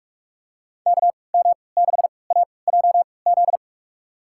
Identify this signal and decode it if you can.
Morse code, heard as a keyed tone, about 42 wpm, spaced as for 25 wpm, 690 Hz. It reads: KM6AJZ
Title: USA Call Signs - 6 Characters - 3x Character Spacing - 45wpm